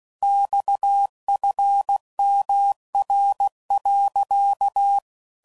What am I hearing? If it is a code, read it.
Morse code, heard as a keyed tone, about 16 wpm, 800 Hz. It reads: XFMR.
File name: CW Code Groups @15WPM / Technical Subjects / PwrSupply